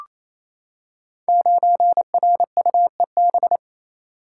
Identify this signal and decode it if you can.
Morse code, heard as a keyed tone, about 28 wpm, 700 Hz. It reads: 9RUEB
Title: Code Groups - 5 Letter:Numbers - 28wpm